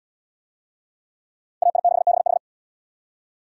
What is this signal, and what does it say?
Morse code, heard as a keyed tone, about 76 wpm, 700 Hz. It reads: AI6BH